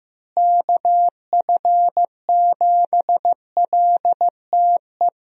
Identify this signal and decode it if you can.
Morse code, heard as a keyed tone, about 15 wpm, 700 Hz. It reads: KF7LTE
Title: USA Call Signs - 6 Characters - Repeated 2x - 15wpm